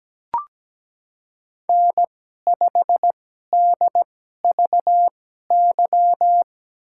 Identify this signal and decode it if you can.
Morse code, heard as a keyed tone, about 17 wpm, spaced as for 12 wpm, 700 Hz. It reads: N5DVY
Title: USA Call Signs - 5 Characters - 2x Character Spacing - 17wpm